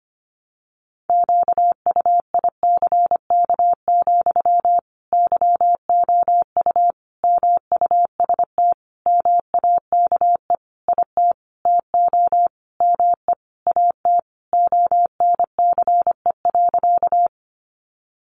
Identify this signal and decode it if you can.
Morse code, heard as a keyed tone, about 25 wpm, 700 Hz. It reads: QUICK, YOU MUST MAKE IT TO ME AT ONCE.
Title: Sentences from Top 500 Words 25wpm